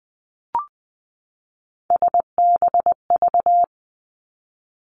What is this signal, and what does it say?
Morse code, heard as a keyed tone, about 20 wpm, 700 Hz. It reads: SBV
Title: Three Challenging Letter-Number Combinations 20wpm